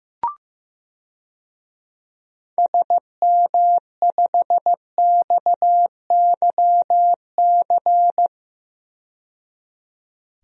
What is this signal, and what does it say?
Morse code, heard as a keyed tone, about 15 wpm, 700 Hz. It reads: SM5XYC
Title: DXCC-100 Call Signs - 6 Characters - 15wpm